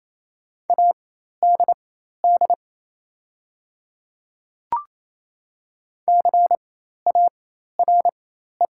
Morse code, heard as a keyed tone, about 27 wpm, spaced as for 13 wpm, 700 Hz. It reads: ADD CARE
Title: 4x Character Spacing - Top 300 Words - 28wpm